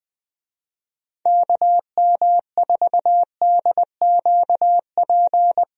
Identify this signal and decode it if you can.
Morse code, heard as a keyed tone, about 20 wpm, 700 Hz. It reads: KM4DQP
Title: USA Call Signs - 6 Characters - Repeated 3x - 20wpm